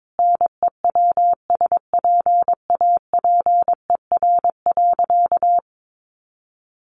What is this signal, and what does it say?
Morse code, heard as a keyed tone, about 22 wpm, 700 Hz. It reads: NEWSPAPER.
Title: Sentences from Top 1400 Words 22wpm